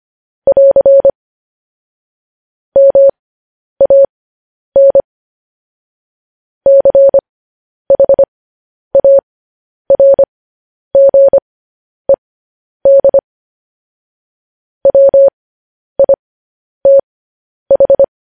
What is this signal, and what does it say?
Morse code, heard as a keyed tone, about 25 wpm, spaced as for 10 wpm, 550 Hz. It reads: <AR> MAN CHARGED WITH
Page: news headlines in morse code at 10 wpm